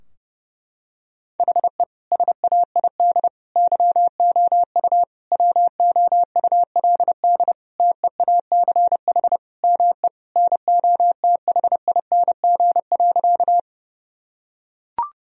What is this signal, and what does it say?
Morse code, heard as a keyed tone, about 32 wpm, 700 Hz. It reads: HE SAID YOU WOULD TEACH ME NOTHING.
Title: Sentences from Top 1000 Words - Review Part 2 - 30wpm